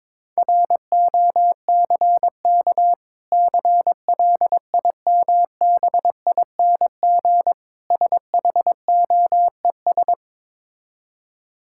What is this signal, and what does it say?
Morse code, heard as a keyed tone, about 22 wpm, 690 Hz. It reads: ROCK CLIMBING SHOES
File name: Difficult Words with Hints - 22wpm